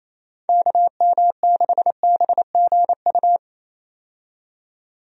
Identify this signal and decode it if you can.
Morse code, heard as a keyed tone, about 28 wpm, 700 Hz. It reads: KM6BGU
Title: USA Call Signs - 6 Characters - 28wpm